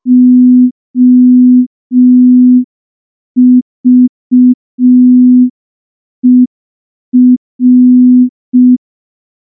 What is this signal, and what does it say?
Morse code, heard as a keyed tone, about 5 wpm, 250 Hz. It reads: OVER
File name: morse / morse-training-05WPM